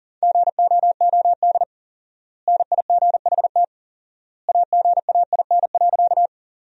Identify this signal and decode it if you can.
Morse code, heard as a keyed tone, about 40 wpm, 700 Hz. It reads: GOOD NIGHT AGAIN.
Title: 4x Word-Spacing - Sentences from Top 300 Words - 40wpm